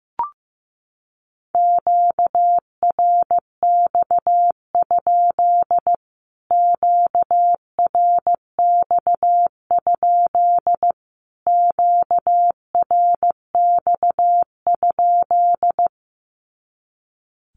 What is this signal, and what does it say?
Morse code, heard as a keyed tone, about 15 wpm, 700 Hz. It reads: QRX? QRX? QRX?
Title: Q-Codes Repeated 3x 15wpm